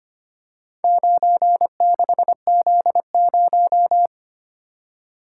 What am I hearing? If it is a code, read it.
Morse code, heard as a keyed tone, about 25 wpm, 700 Hz. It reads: 96Z0
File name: practice / 25wpm / Four Letter-Number Combinations - 25wpm